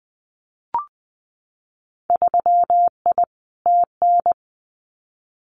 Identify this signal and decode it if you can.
Morse code, heard as a keyed tone, about 20 wpm, 700 Hz. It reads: 3I TN
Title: Winter Field Day Exchanges 20wpm